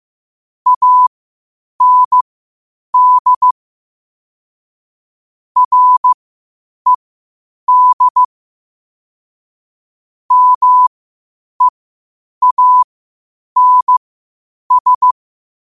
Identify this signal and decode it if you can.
Morse code, heard as a keyed tone, about 15 wpm, spaced as for 8 wpm, 1000 Hz. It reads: AND RED MEANS
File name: qotdfiles / 8wpm